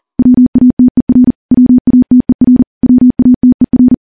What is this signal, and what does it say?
Morse code, heard as a keyed tone, about 40 wpm, 250 Hz. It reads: WATER WATER WATER
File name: morse / morse-training-40WPM